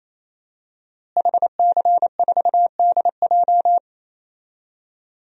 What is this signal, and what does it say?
Morse code, heard as a keyed tone, about 27 wpm, 700 Hz. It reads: HC4DJ
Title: DXCC-100 Call Signs - 5 Characters - Repeated 3x - 28wpm